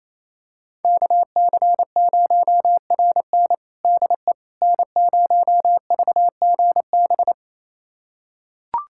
Morse code, heard as a keyed tone, about 28 wpm, 700 Hz. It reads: KC0RN DE N0VGB
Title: USA Callsign DE Callsign 28wpm